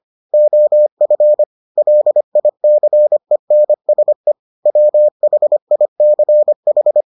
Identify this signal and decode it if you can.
Morse code, heard as a keyed tone, about 25 wpm, 600 Hz. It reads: OF LICENSE WHICH